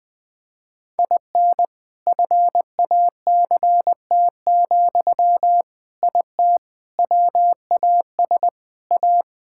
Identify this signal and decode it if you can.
Morse code, heard as a keyed tone, about 20 wpm, 700 Hz. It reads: IN FACT, IT WAS A
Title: Sentences from Top 1000 Words - Review Part 6 - 20wpm